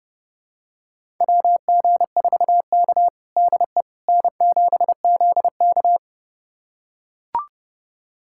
Morse code, heard as a keyed tone, about 32 wpm, 700 Hz. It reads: WG4K DE N7ZK
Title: USA Callsign DE Callsign - 4 Characters - 30wpm